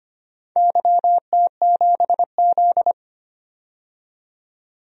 Morse code, heard as a keyed tone, about 25 wpm, 700 Hz. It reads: YT7Z
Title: DXCC-100 Call Signs - 4 Characters - 25wpm